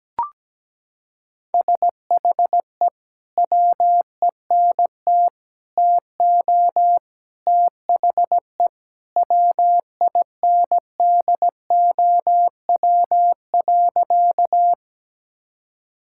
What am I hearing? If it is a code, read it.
Morse code, heard as a keyed tone, about 17 wpm, 700 Hz. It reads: SHE WENT TO THE WINDOW.
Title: Sentences from Top 1600 Words 17wpm